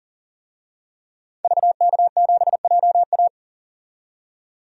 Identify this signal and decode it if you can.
Morse code, heard as a keyed tone, about 38 wpm, 700 Hz. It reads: VK7JA
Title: DXCC-100 Call Signs - 5 Characters - 40wpm